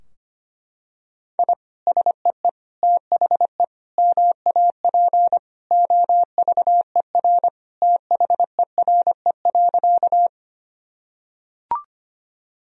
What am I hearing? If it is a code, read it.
Morse code, heard as a keyed tone, about 25 wpm, 700 Hz. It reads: I SEE THE MAP OVER THERE.
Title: Sentences from Top 1000 Words - Review Part 2 - 25wpm